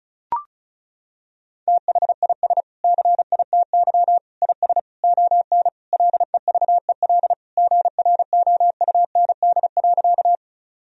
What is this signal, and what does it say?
Morse code, heard as a keyed tone, about 34 wpm, 700 Hz. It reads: THIS CITY IS ON LEVEL GROUND.
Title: Sentences from Top 1000 Words - Review Part 4 - 35wpm